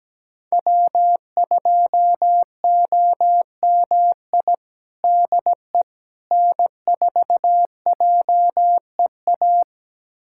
Morse code, heard as a keyed tone, about 17 wpm, 700 Hz. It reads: W2OMI DE N4JEA
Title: USA Callsign DE Callsign - 5 Characters - 17wpm